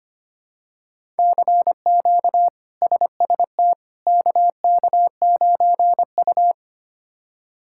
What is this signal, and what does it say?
Morse code, heard as a keyed tone, about 25 wpm, 700 Hz. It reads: CQ SST KK9U
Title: K1USN SST QSO Practice - 25wpm